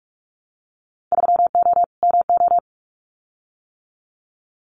Morse code, heard as a keyed tone, about 46 wpm, 700 Hz. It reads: 3O MO
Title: Winter Field Day Exchanges Encore 45wpm